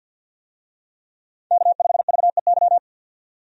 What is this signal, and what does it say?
Morse code, heard as a keyed tone, about 50 wpm, 700 Hz. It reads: K5VEY
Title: USA Call Signs - 5 Characters - 50wpm